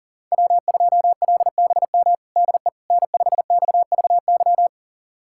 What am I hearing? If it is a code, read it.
Morse code, heard as a keyed tone, about 40 wpm, 700 Hz. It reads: W2LBM DE N5XVY
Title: USA Callsign DE Callsign 40wpm